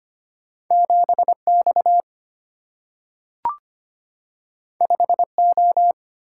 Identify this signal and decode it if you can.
Morse code, heard as a keyed tone, about 25 wpm, 700 Hz. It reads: 7X 5O
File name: Two Letter-Number Combinations - 25wpm